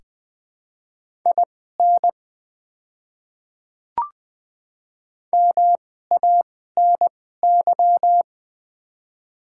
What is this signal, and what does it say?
Morse code, heard as a keyed tone, about 20 wpm, spaced as for 14 wpm, 700 Hz. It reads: IN MANY
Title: 2x Character Spacing - Top 200 Words - 20wpm